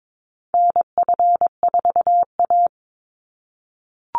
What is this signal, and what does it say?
Morse code, heard as a keyed tone, about 22 wpm, 700 Hz. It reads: NF4A
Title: Call Signs - 22wpm